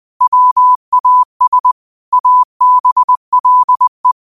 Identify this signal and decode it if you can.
Morse code, heard as a keyed tone, about 20 wpm, 1000 Hz. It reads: WAS ABLE
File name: qotdfiles / 20wpm